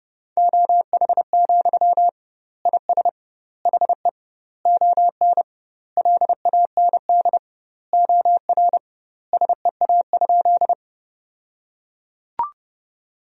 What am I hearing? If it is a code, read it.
Morse code, heard as a keyed tone, about 32 wpm, 700 Hz. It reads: OH, IS HE ON LAND OR SEA?
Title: 2x Word-Spacing - Sentences from Top 500 Words - 30wpm